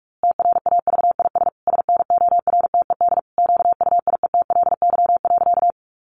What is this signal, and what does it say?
Morse code, heard as a keyed tone, about 44 wpm, 700 Hz. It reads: TRAVIS SNORTED QUIETLY.
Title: A Sound Of Thunder 45wpm